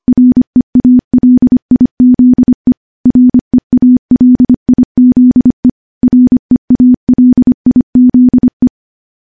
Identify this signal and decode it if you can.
Morse code, heard as a keyed tone, about 25 wpm, 250 Hz. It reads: REALIZE REALIZE REALIZE
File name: morse / morse-training-25WPM